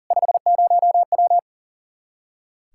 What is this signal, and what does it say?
Morse code, heard as a keyed tone, about 40 wpm, 700 Hz. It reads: 50W
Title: Practice QSO - Exchange 2 - Info - 40wpm